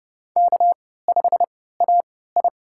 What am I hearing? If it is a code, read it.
Morse code, heard as a keyed tone, about 31 wpm, spaced as for 17 wpm, 700 Hz. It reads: K5AI
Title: USA Call Signs - 4 Characters - 3x Character Spacing - 30wpm